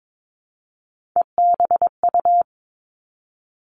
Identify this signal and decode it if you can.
Morse code, heard as a keyed tone, about 22 wpm, 700 Hz. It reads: EBU
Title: Three Challenging Letter-Number Combinations 22wpm